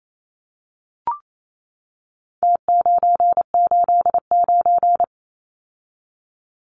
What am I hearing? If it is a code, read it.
Morse code, heard as a keyed tone, about 28 wpm, 700 Hz. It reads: T989
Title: Numbers - Contest Style - 4-Digit 28wpm